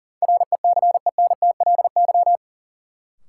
Morse code, heard as a keyed tone, about 37 wpm, 700 Hz. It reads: RECENTLY
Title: Sentences from Top 1200 Words - New Word Drill - 40wpm